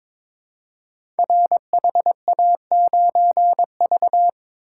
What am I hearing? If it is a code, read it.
Morse code, heard as a keyed tone, about 22 wpm, 700 Hz. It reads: RHA9V
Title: Code Groups - 5 Letter:Numbers - 22wpm